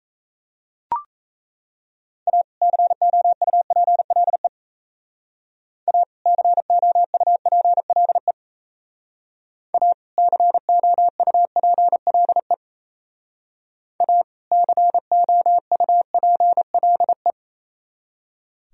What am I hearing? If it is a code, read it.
Morse code, heard as a keyed tone, about 34 wpm, 700 Hz. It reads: A COUPLE A COUPLE A COUPLE A COUPLE
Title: Speed-Racing - Sets of 2 Words 28wpm